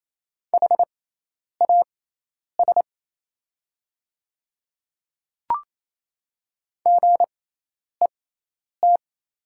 Morse code, heard as a keyed tone, about 28 wpm, spaced as for 10 wpm, 700 Hz. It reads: HAS GET
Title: 6x Character Spacing - Top 200 Words - 28wpm